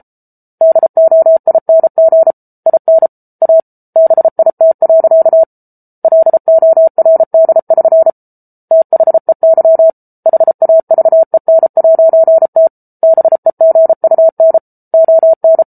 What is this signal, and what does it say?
Morse code, heard as a keyed tone, about 34 wpm, 650 Hz. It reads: DOING IN A BIT. LORD<SN> THEY HAVEN'T BEGUN ON